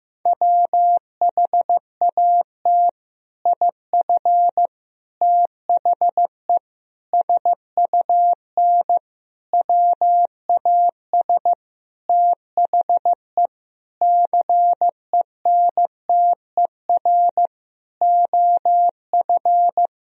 Morse code, heard as a keyed tone, about 15 wpm, 700 Hz. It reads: WHAT IF THE SUN WAS THE CENTER OF
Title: Sentences from Top 1000 Words - Review Part 2 - 15wpm